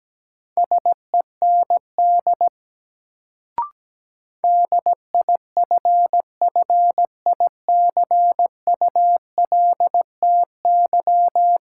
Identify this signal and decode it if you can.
Morse code, heard as a keyed tone, about 17 wpm, 700 Hz. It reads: SEND DIFFICULTY
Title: Sentences from Top 1500 Words - New Word Drill - 17wpm